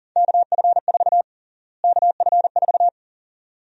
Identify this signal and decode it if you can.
Morse code, heard as a keyed tone, about 37 wpm, 700 Hz. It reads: KF4 KF4
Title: USA Call Sign Prefixes - Repeated 2x - 40wpm